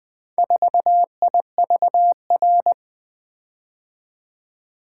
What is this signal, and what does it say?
Morse code, heard as a keyed tone, about 24 wpm, 700 Hz. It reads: 4IVR
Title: Speed-Racing - Four Letter-Number Combinations - 20wpm